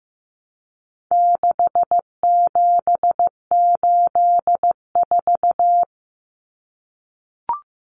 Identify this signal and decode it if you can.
Morse code, heard as a keyed tone, about 15 wpm, 700 Hz. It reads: 6784
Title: Common Cut Numbers - 4-Digit 15wpm